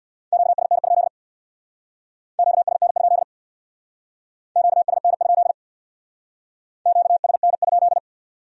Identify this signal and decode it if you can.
Morse code, heard as a keyed tone, about 66 wpm, 700 Hz. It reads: QSN? QSN? QSN? QSN?